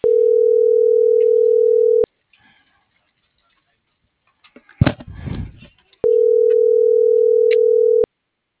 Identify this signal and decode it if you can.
Ringback tone (North America), 440 + 480 Hz.